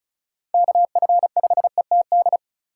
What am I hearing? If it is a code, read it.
Morse code, heard as a keyed tone, about 36 wpm, 700 Hz. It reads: KF5ETD
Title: K1USN SST QSO Practice - 35wpm